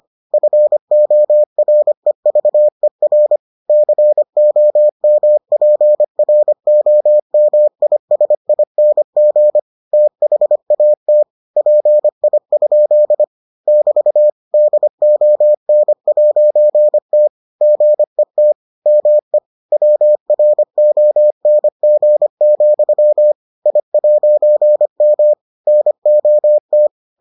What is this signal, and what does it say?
Morse code, heard as a keyed tone, about 25 wpm, 600 Hz. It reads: FOREVER COMPROMISING THAT PI? <BT> DON'T GET ME WRONG, I'M NOT